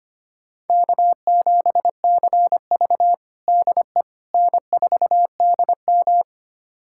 Morse code, heard as a keyed tone, about 25 wpm, 700 Hz. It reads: K7CV DE N4DM
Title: USA Callsign DE Callsign - 4 Characters - 25wpm